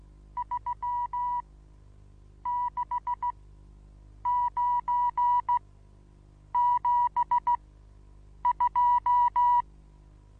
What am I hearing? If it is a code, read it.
Morse code, heard as a keyed tone, about 16 wpm, spaced as for 7 wpm, 970 Hz. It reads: 36972
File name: Onlinelehrgang / morsen / DJ4UF Morsen25